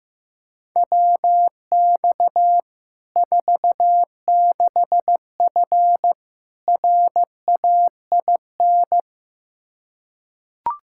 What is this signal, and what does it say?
Morse code, heard as a keyed tone, about 16 wpm, 700 Hz. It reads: WX 46F RAIN
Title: Practice QSO - Exchange 2 - Info - 15wpm